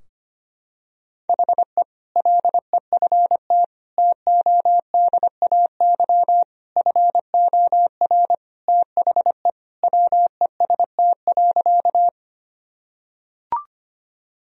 Morse code, heard as a keyed tone, about 25 wpm, 700 Hz. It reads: HE LEFT TODAY FOR THE WEST.
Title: Sentences from Top 1000 Words - Review Part 2 - 25wpm